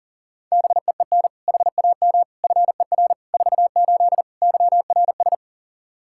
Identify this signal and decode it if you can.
Morse code, heard as a keyed tone, about 40 wpm, 700 Hz. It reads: BEEN HAM FER 48 YRS
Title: Practice QSO - Exchange 3 - More Info - 40wpm